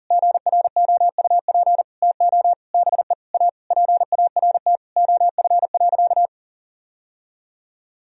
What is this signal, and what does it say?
Morse code, heard as a keyed tone, about 40 wpm, 700 Hz. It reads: GROUP TO BE A PART OF.
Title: Sentences from Top 1000 Words - Review Part 1 - 40wpm